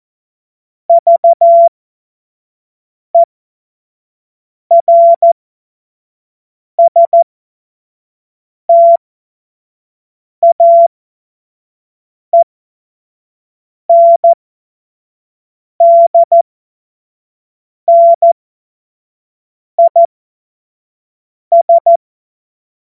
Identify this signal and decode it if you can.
Morse code, heard as a keyed tone, about 14 wpm, spaced as for 5 wpm, 680 Hz. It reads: VERSTAENDNIS
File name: Witz-21-Oktober-2025-25-F70